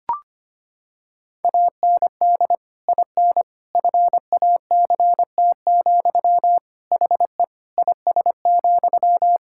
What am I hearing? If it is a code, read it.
Morse code, heard as a keyed tone, about 25 wpm, 700 Hz. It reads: AND IN FACT, HE IS,
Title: Sentences from Top 1600 Words 25wpm